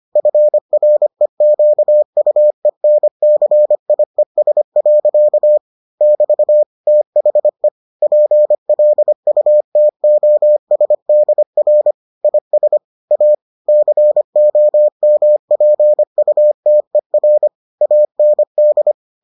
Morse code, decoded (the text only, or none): FREQUENCIES. <BT> THE PLUTOSDR IS A COMPUTER AND